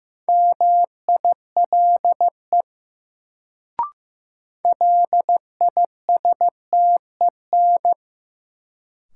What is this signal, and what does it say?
Morse code, heard as a keyed tone, about 15 wpm, 700 Hz. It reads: MILE LISTEN
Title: Sentences from Top 400 Words - New Word Drill - 15wpm